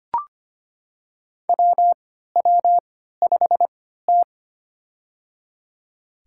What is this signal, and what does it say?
Morse code, heard as a keyed tone, about 25 wpm, spaced as for 14 wpm, 700 Hz. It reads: WW5T
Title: USA Call Signs - 4 Characters - 3x Character Spacing - 25wpm